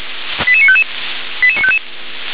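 Five-tone selective call (ZVEI), heard as quick runs of five tones, tones of about 70 ms. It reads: 88955 88955
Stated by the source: Received on SDR as ZVEI five-tone selective call.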